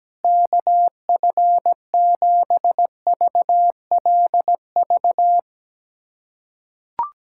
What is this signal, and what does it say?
Morse code, heard as a keyed tone, about 17 wpm, 700 Hz. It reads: KF7VLV
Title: USA Callsign DE Callsign 17wpm